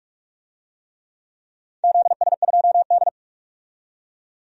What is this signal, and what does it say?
Morse code, heard as a keyed tone, about 42 wpm, 700 Hz. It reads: ZS2D
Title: DXCC-100 Call Signs - 4 Characters - 45wpm